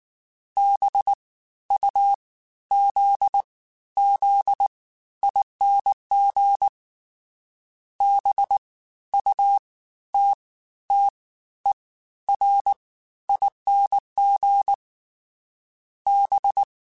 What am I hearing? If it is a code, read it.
Morse code, heard as a keyed tone, about 19 wpm, spaced as for 12 wpm, 800 Hz. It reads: BUZZING BUTTERING B
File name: Special INGs-mp3s / Special INGs-mp3s / Fast ing @18